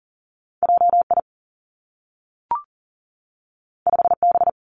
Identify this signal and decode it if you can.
Morse code, heard as a keyed tone, about 40 wpm, 700 Hz. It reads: J I 5 B
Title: Two Challenging Letter-Number Combinations 40wpm